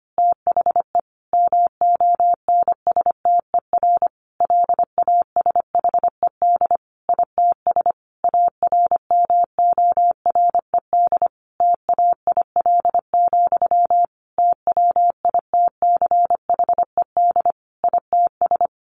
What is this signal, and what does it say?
Morse code, heard as a keyed tone, about 25 wpm, 700 Hz. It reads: THE MONSTER LASHED ITS ARMORED TAIL, TWITCHED ITS
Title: A Sound Of Thunder 25wpm